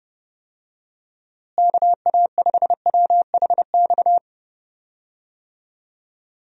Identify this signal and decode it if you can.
Morse code, heard as a keyed tone, about 32 wpm, 700 Hz. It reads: KA5WHX
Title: USA Call Signs - 6 Characters - Repeated 3x - 30wpm